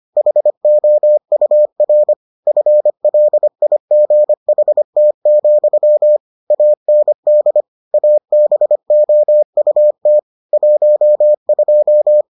Morse code, heard as a keyed tone, about 25 wpm, 600 Hz. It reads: HOUR FLIGHT, AND ABOUT 12